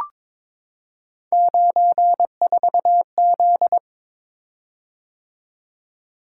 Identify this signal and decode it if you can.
Morse code, heard as a keyed tone, about 22 wpm, 700 Hz. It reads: 94Z
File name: practice / 22wpm / Three Letter-Number Combinations - 22wpm